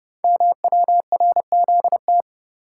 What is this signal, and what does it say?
Morse code, heard as a keyed tone, about 30 wpm, 700 Hz. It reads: MWRZT
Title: Code Groups - 5 Letters - 30wpm